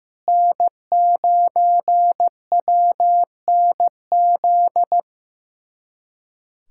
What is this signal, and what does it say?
Morse code, heard as a keyed tone, about 16 wpm, 700 Hz. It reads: N9WNZ
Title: USA Callsign DE Callsign - 5 Characters - 15wpm